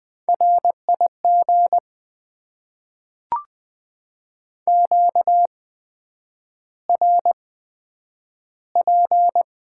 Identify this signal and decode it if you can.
Morse code, heard as a keyed tone, about 20 wpm, spaced as for 9 wpm, 700 Hz. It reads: RIG QRP
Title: Top 90 QSO Elements - 8x Character Spacing - 20wpm